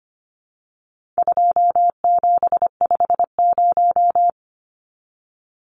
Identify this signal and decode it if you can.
Morse code, heard as a keyed tone, about 25 wpm, 700 Hz. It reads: 2750
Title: Numbers - 4-Digit 25wpm